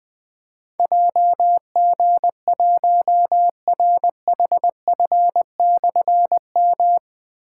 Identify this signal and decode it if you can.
Morse code, heard as a keyed tone, about 20 wpm, 700 Hz. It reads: JG1RHF/M
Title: Difficult Call Signs 20wpm